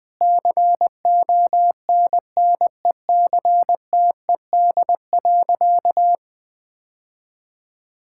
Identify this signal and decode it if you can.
Morse code, heard as a keyed tone, about 21 wpm, 700 Hz. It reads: CONNECTED.
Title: Easy Sentences with Hints - 20wpm